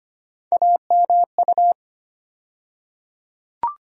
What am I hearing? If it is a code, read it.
Morse code, heard as a keyed tone, about 25 wpm, 700 Hz. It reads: AMU
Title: USA Call Sign Postfixes - 3 Letters 25wpm